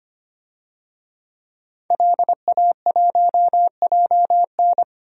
Morse code, heard as a keyed tone, about 25 wpm, 700 Hz. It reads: LA1JN
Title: DXCC-100 Call Signs - 5 Characters - 25wpm